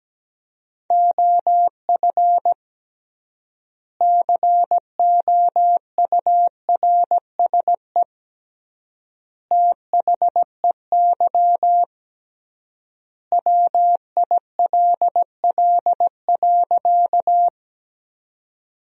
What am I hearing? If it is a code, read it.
Morse code, heard as a keyed tone, about 17 wpm, 700 Hz. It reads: OF COURSE THEY WILL.
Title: 3x Word-Spacing - Sentences from Top 500 Words - 17wpm